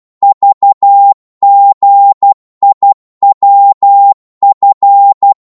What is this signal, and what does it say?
Morse code, heard as a keyed tone, about 12 wpm, 800 Hz. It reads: VGIWF